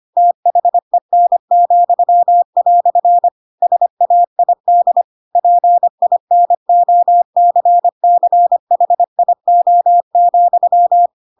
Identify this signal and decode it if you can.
Morse code, heard as a keyed tone, about 25 wpm, 700 Hz. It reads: THEN," SAID PINOCCHIO,
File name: Pinocchio / Pinocchio ch7 25 25 ik1hge